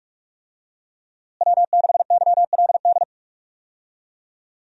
Morse code, heard as a keyed tone, about 44 wpm, 700 Hz. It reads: W6YLD
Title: USA Call Signs - 5 Characters - Repeated 3x - 45wpm